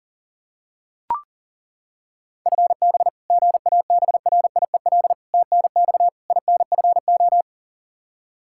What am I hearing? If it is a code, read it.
Morse code, heard as a keyed tone, about 37 wpm, 700 Hz. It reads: FB GABRIEL TNX INFO
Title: Practice QSO - Exchange 3 - More Info - 40wpm